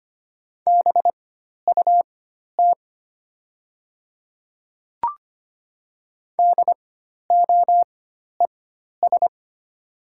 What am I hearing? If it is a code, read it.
Morse code, heard as a keyed tone, about 25 wpm, spaced as for 12 wpm, 700 Hz. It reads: BUT DOES
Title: 4x Character Spacing - Top 300 Words - 25wpm